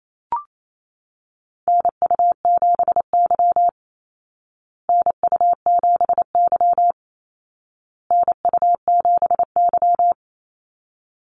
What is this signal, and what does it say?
Morse code, heard as a keyed tone, about 28 wpm, 700 Hz. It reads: NU7Y NU7Y NU7Y
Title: Call Signs Repeated 2x - 28wpm